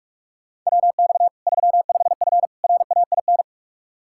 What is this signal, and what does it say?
Morse code, heard as a keyed tone, about 46 wpm, 700 Hz. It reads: WX 35F RAIN
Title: K1USN SST QSO Practice 45wpm